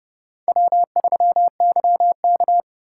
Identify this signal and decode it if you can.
Morse code, heard as a keyed tone, about 26 wpm, 700 Hz. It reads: W3YK